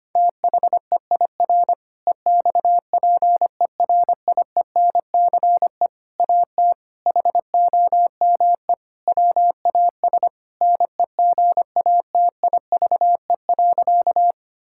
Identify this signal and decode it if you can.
Morse code, heard as a keyed tone, about 25 wpm, 700 Hz. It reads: THEIR EXPERIENCE AT HOME WAS NEGATIVE.
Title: Sentences from Top 1000 Words - Review Part 6 - 25wpm